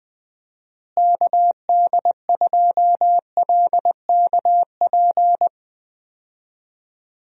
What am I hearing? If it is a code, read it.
Morse code, heard as a keyed tone, about 20 wpm, 700 Hz. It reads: KD2LKP
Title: USA Call Signs - 6 Characters - Repeated 3x - 20wpm